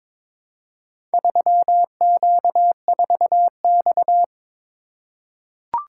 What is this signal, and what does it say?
Morse code, heard as a keyed tone, about 22 wpm, 700 Hz. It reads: 3Q4X
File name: practice / 22wpm / Four Letter-Number Combinations - 22wpm